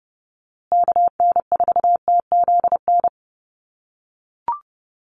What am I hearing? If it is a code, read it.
Morse code, heard as a keyed tone, about 30 wpm, 700 Hz. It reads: KN4TZN
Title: USA Call Signs Repeated 2x 30wpm